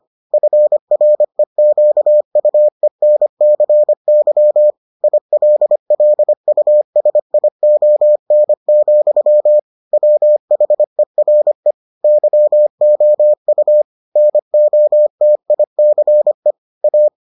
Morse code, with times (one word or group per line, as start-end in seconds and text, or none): 0.3-4.7 FREQUENCY
5.0-9.6 ILLUSION,
9.9-11.7 WHERE
12.0-13.8 YOU
14.2-16.5 NOTICE
16.8-17.1 A